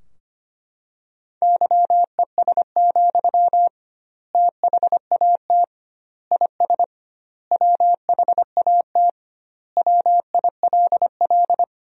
Morse code, heard as a keyed tone, about 25 wpm, 700 Hz. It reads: YES, THAT IS WHAT WILL
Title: 2x Word-Spacing - Sentences from Top 500 Words - 25wpm